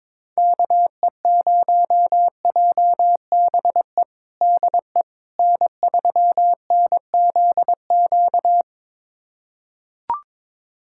Morse code, heard as a keyed tone, about 22 wpm, 700 Hz. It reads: KE0JBE DE N3NZQ
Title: USA Callsign DE Callsign 22wpm